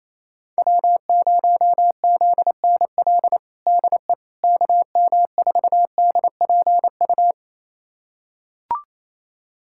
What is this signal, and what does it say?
Morse code, heard as a keyed tone, about 28 wpm, 700 Hz. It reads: W0ZNL DE KM4DPU
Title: USA Callsign DE Callsign 28wpm